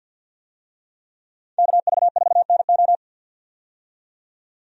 Morse code, heard as a keyed tone, about 49 wpm, 700 Hz. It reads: KV4NO
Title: USA Call Signs - 5 Characters - 50wpm